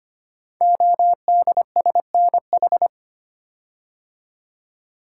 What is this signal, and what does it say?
Morse code, heard as a keyed tone, about 25 wpm, 700 Hz. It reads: ODSNH
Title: Code Groups - 5 Letter:Numbers - 25wpm